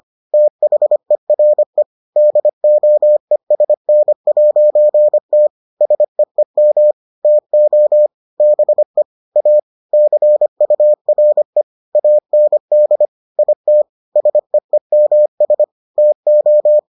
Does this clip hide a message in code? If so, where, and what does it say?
Yes: Morse code from 0.3 to 16.8 s, heard as a keyed tone, about 25 wpm, 600 Hz. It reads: THERE DOESN'T SEEM TO BE A CURE AND IT SEEMS TO